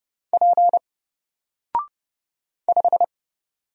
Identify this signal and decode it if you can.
Morse code, heard as a keyed tone, about 30 wpm, 700 Hz. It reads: P 5
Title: Single Letter-Number 30wpm